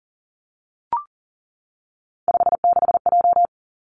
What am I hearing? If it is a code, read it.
Morse code, heard as a keyed tone, about 41 wpm, 700 Hz. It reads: 56J
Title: Three Challenging Letter-Number Combinations 40wpm